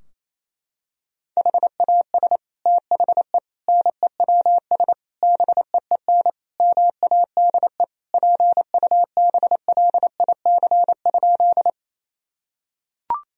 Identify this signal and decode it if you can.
Morse code, heard as a keyed tone, about 28 wpm, 700 Hz. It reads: HAS THE NEWS BEEN MADE PUBLIC?
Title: Sentences from Top 1000 Words - Review Part 3 - 28wpm